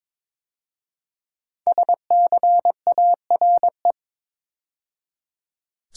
Morse code, heard as a keyed tone, about 22 wpm, 700 Hz. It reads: SCARE
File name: 5 Letter Words - First 3 Letters Given - 22wpm